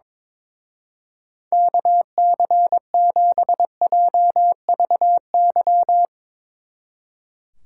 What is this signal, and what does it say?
Morse code, heard as a keyed tone, about 24 wpm, 700 Hz. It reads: KC7JVY
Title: Speed Racing - USA 2x3 Call Signs 22wpm